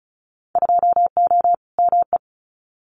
Morse code, heard as a keyed tone, about 34 wpm, 690 Hz. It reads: 2O ME
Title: Winter Field Day Exchanges 35wpm